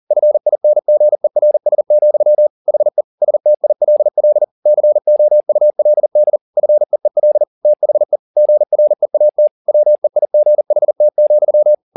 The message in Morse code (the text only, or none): FINGERS, HE STILL COULD FEEL THE GREAT WEIGHT,